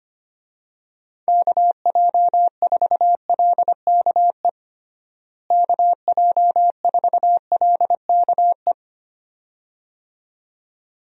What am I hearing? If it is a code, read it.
Morse code, heard as a keyed tone, about 25 wpm, 700 Hz. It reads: KJ4LKE KJ4LKE
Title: USA Call Signs - 6 Characters - Repeated 2x - 25wpm